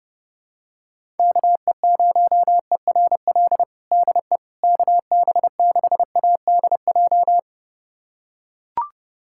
Morse code, heard as a keyed tone, about 32 wpm, 700 Hz. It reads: KE0ERL DE KB6ADJ
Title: USA Callsign DE Callsign 30wpm